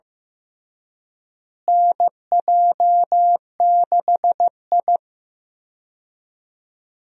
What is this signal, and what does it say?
Morse code, heard as a keyed tone, about 14 wpm, 700 Hz. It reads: NJ6I
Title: USA Call Signs - 4 Characters - Repeated 2x - 15wpm